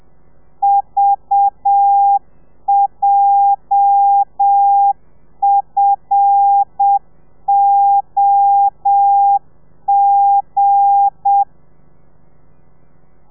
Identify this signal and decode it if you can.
Morse code, heard as a keyed tone, about 7 wpm, 800 Hz. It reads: VJFOG
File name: telegrafia / i2tfj / Lez8 35